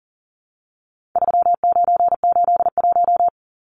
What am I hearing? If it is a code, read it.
Morse code, heard as a keyed tone, about 40 wpm, 700 Hz. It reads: 3981
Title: Numbers - 4-Digit 40wpm